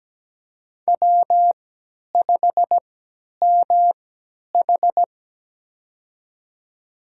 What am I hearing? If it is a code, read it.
Morse code, heard as a keyed tone, about 17 wpm, spaced as for 10 wpm, 700 Hz. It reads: W5MH
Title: USA Call Signs - 4 Characters - 3x Character Spacing - 17wpm